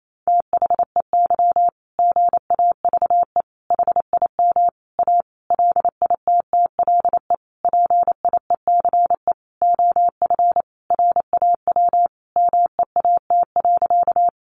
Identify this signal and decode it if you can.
Morse code, heard as a keyed tone, about 28 wpm, 700 Hz. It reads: THEY GAVE HIM A LITTLE PIECE OF RAW MEAT.
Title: Sentences from Top 1500 Words 28wpm